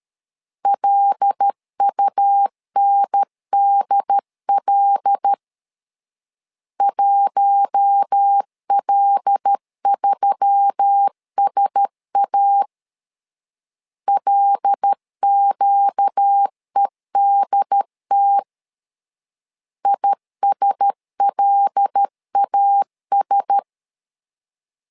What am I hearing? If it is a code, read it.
Morse code, heard as a keyed tone, about 13 wpm, 800 Hz. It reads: LUNDL 1L3SA LQEDT ISLAS
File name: LEZIONI / 5 LBVC / 5 L 12wpm 1913